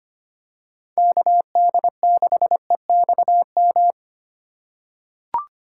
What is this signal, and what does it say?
Morse code, heard as a keyed tone, about 25 wpm, 700 Hz. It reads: KD6EXM
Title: USA Call Signs - 6 Characters - 2x Character Spacing - 25wpm